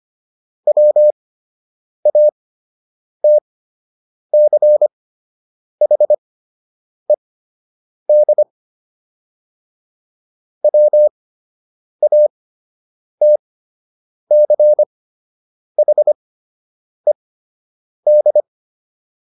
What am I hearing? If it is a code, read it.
Morse code, heard as a keyed tone, about 25 wpm, spaced as for 8 wpm, 600 Hz. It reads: WATCHED WATCHED